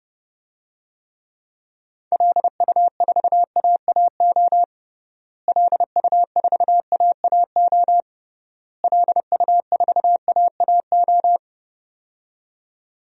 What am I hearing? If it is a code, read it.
Morse code, heard as a keyed tone, about 32 wpm, 700 Hz. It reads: LU4AAO LU4AAO LU4AAO
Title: DXCC-100 Call Signs - 6 Characters - Repeated 3x - 30wpm